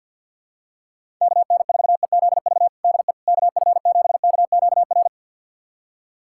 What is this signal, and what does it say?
Morse code, heard as a keyed tone, about 50 wpm, 700 Hz. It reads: KN4EZV DE KF7KQR
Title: USA Callsign DE Callsign 50wpm